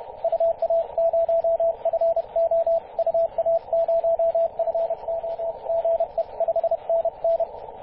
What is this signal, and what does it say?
Morse code, heard as a keyed tone, about 31 wpm, 670 Hz, with radio noise. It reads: UA0FO UA0FO GE 5NN
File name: 8q7fu-24-896mhz-cw